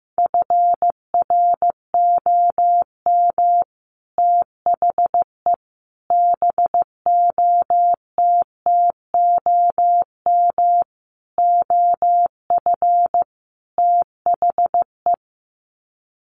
Morse code, heard as a keyed tone, about 15 wpm, 700 Hz. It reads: FROM THE BOTTOM OF THE
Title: Sets of 5 Words Encore2 15wpm